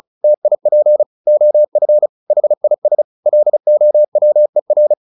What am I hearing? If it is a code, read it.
Morse code, heard as a keyed tone, about 34 wpm, 600 Hz. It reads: TIP OF HIS LOWER
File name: the old man and the sea 2 35wpm